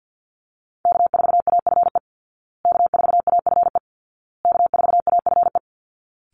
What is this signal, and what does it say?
Morse code, heard as a keyed tone, about 50 wpm, 700 Hz. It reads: K4AFE K4AFE K4AFE
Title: Call Signs Repeated 2x - 50wpm